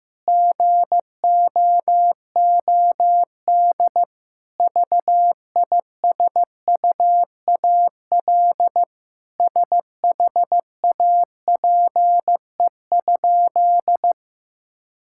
Morse code, heard as a keyed tone, about 15 wpm, 700 Hz. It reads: GOOD VISUAL SHAPE?
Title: Sentences from Top 1000 Words - Review Part 6 - 15wpm